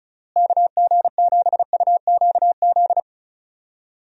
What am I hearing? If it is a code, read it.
Morse code, heard as a keyed tone, about 34 wpm, 700 Hz. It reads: KG7UQZ
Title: USA Call Signs - 6 Characters - Repeated 2x - 35wpm